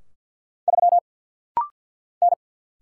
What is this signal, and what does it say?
Morse code, heard as a keyed tone, about 49 wpm, 700 Hz. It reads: 3 N
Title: Single Letter-Number Alt - Rapid-Fire 50wpm